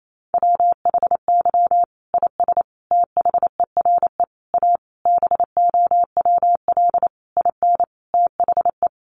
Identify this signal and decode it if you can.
Morse code, heard as a keyed tone, about 28 wpm, 700 Hz. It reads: WHY IS THERE A BOWL IN THE
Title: Sentences from Top 1200 Words 28wpm